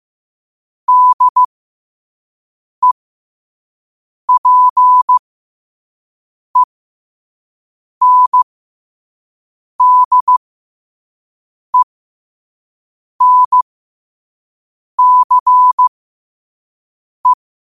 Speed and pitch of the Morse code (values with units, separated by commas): 15 wpm, spaced as for 5 wpm, 1000 Hz